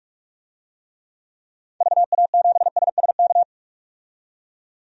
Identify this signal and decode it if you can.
Morse code, heard as a keyed tone, about 45 wpm, 700 Hz. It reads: VA7SSK
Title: DXCC-100 Call Signs - 6 Characters - 45wpm